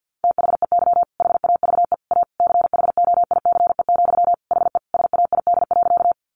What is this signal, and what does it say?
Morse code, heard as a keyed tone, about 48 wpm, 700 Hz. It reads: THEY HAVE A CHOICE, HE SAID.
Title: Sentences from Top 900 Words 50wpm